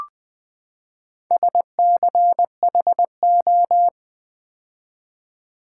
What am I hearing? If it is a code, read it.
Morse code, heard as a keyed tone, about 20 wpm, 700 Hz. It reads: SCHO
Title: Build A Word - 4 Letters At A Time - 11-Letter Words - 20wpm